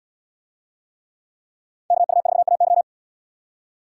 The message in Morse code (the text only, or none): KU4AY